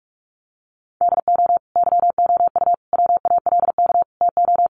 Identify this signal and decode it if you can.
Morse code, heard as a keyed tone, about 44 wpm, 700 Hz. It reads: DO YOU WALK TO
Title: Sentences from Top 1400 Words 45wpm